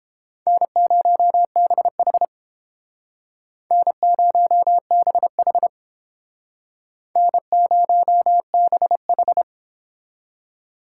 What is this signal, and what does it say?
Morse code, heard as a keyed tone, about 32 wpm, 700 Hz. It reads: N0BH N0BH N0BH